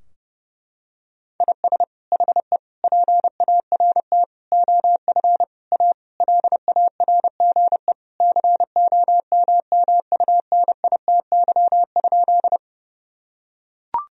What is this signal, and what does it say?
Morse code, heard as a keyed tone, about 32 wpm, 700 Hz. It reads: IS HE PART OF A LARGE COMMUNITY?
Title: Sentences from Top 1000 Words - Review Part 4 - 30wpm